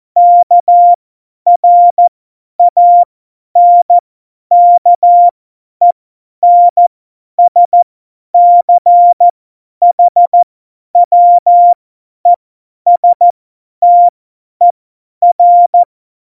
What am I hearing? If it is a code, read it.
Morse code, heard as a keyed tone, about 14 wpm, spaced as for 10 wpm, 700 Hz. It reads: KRANKENSCHWESTER
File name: Morsekurs / Witze / Witz 13 March 2018, 50 F70